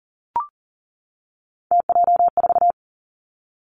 Morse code, heard as a keyed tone, about 39 wpm, 700 Hz. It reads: TJ4
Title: Three Challenging Letter-Number Combinations 40wpm